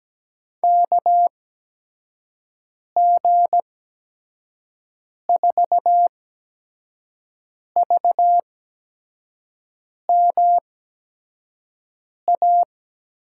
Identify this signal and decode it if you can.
Morse code, heard as a keyed tone, about 17 wpm, spaced as for 5 wpm, 700 Hz. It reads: KG4VMA
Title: USA Call Signs - 6 Characters - 8x Character Spacing - 17wpm